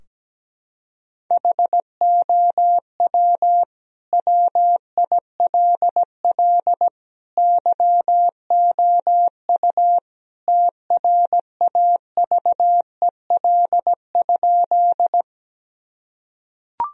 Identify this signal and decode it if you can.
Morse code, heard as a keyed tone, about 17 wpm, 700 Hz. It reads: HOW WILL YOU TRAVEL?
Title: Sentences from Top 1000 Words - Review Part 2 - 17wpm